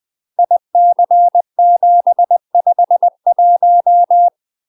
Morse code, heard as a keyed tone, about 20 wpm, 700 Hz. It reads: IC751
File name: UK MASTER QSOs TRAINING/20WPM